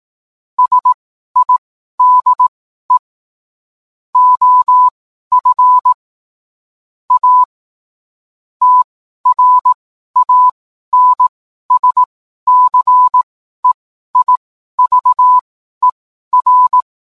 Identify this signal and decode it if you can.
Morse code, heard as a keyed tone, about 18 wpm, spaced as for 12 wpm, 1000 Hz. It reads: SIDE OF A TRANSCEIVER